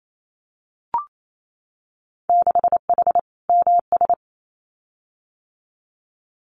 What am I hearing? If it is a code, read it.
Morse code, heard as a keyed tone, about 28 wpm, 700 Hz. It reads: 6H MS
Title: Winter Field Day Exchanges Encore 28wpm